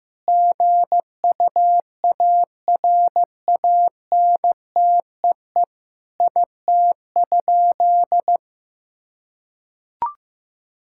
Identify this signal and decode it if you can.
Morse code, heard as a keyed tone, about 15 wpm, 700 Hz. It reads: GUARANTEE IT?
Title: Sentences from Top 1600 Words 15wpm